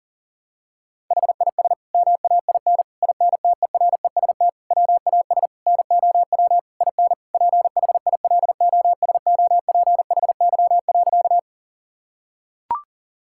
Morse code, heard as a keyed tone, about 40 wpm, 700 Hz. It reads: HIS MAIN INTEREST WAS NOW IN PHILOSOPHY.
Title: Sentences from Top 1000 Words - Review Part 6 - 40wpm